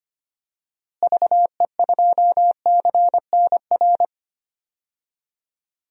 Morse code, heard as a keyed tone, about 25 wpm, 700 Hz. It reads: VE2CNR
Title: DXCC-100 Call Signs - 6 Characters - Repeated 3x - 25wpm